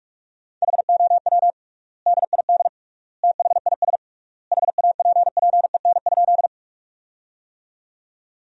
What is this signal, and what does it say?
Morse code, heard as a keyed tone, about 41 wpm, 700 Hz. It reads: HOW DID THIS HAPPEN?
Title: 3x Word-Spacing - Sentences from Top 400 Words - 45wpm